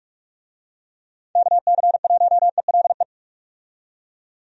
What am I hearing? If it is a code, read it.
Morse code, heard as a keyed tone, about 46 wpm, 700 Hz. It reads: KC1ELE
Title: USA Call Signs - 6 Characters - 45wpm